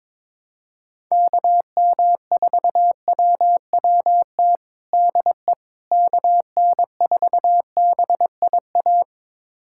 Morse code, heard as a keyed tone, about 22 wpm, 700 Hz. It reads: KM4WWT DE KN4BIA
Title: USA Callsign DE Callsign 22wpm